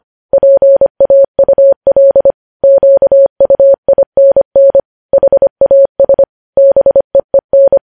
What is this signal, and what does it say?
Morse code, heard as a keyed tone, about 25 wpm, 550 Hz. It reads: PAUL QUINN HAS BEEN
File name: podcast / News headlines-1x-25x25